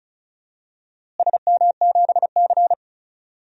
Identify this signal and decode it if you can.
Morse code, heard as a keyed tone, about 36 wpm, 700 Hz. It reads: SM7C
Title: DXCC-100 Call Signs - 4 Characters - Repeated 3x - 35wpm